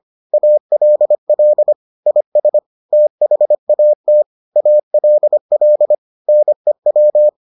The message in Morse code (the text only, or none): ALL IS THAT ALL NEW